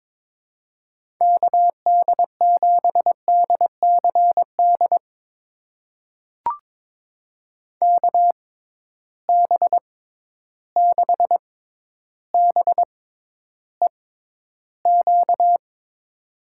Morse code, heard as a keyed tone, about 22 wpm, spaced as for 11 wpm, 700 Hz. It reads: KD7DCD KB6BEQ
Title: USA Call Signs - 6 Characters - 6x Character Spacing - 22wpm